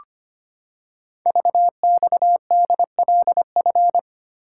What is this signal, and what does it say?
Morse code, heard as a keyed tone, about 25 wpm, 700 Hz. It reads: VXDLF
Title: Code Groups - 5 Letters - 25wpm